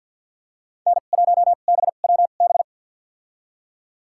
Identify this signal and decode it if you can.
Morse code, heard as a keyed tone, about 47 wpm, spaced as for 35 wpm, 700 Hz. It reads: N1BWB
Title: USA Call Signs - 5 Characters - 2x Character Spacing - 50wpm